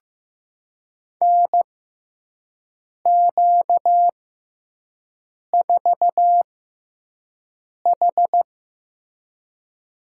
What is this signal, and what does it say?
Morse code, heard as a keyed tone, about 15 wpm, spaced as for 5 wpm, 700 Hz. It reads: NQ4H
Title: USA Call Signs - 4 Characters - 6x Character Spacing - 15wpm